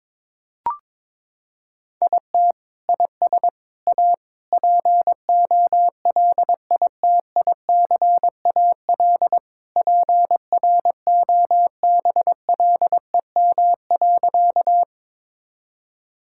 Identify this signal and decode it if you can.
Morse code, heard as a keyed tone, about 22 wpm, 700 Hz. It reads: IT IS A POLITICAL PROBLEM.
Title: Sentences from Top 1000 Words - Review Part 5 - 22wpm